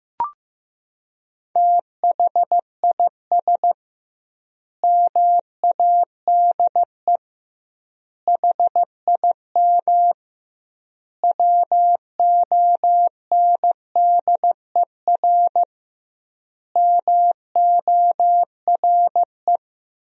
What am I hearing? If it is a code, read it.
Morse code, heard as a keyed tone, about 15 wpm, 700 Hz. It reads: THIS MADE HIM WONDER MORE
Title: 2x Word-Spacing - Sentences from Top 500 Words - 15wpm